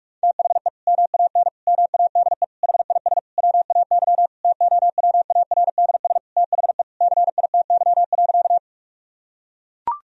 Morse code, heard as a keyed tone, about 42 wpm, 700 Hz. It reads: THE MAN MADE HIS WAY TOWARDS THE CITY.
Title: Sentences from Top 1000 Words - Review Part 2 - 45wpm